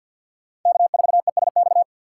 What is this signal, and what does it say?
Morse code, heard as a keyed tone, about 48 wpm, 700 Hz. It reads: K4ESX
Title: USA Call Signs - 3x Character Spacing - 50wpm